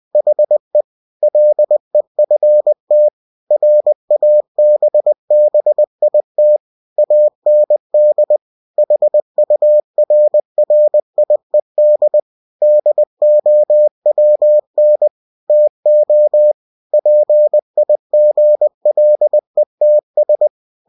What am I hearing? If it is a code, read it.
Morse code, heard as a keyed tone, about 20 wpm, 600 Hz. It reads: HE LEFT RABBIT AND HURRIED DOWN TO PIGLETS
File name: cw-books / winnie-the-pooh / winnie-20wpm-page-0078